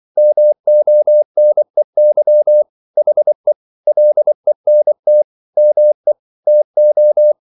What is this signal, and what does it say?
Morse code, heard as a keyed tone, about 24 wpm, 600 Hz. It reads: MONEY HE LENT ME TO